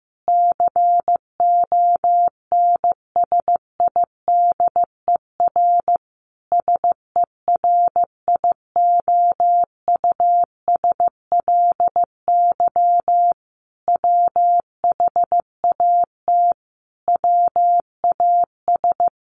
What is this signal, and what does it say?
Morse code, heard as a keyed tone, about 15 wpm, 700 Hz. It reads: CONSIDER SERIOUSLY WHAT WAS
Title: Sentences from Top 1200 Words 15wpm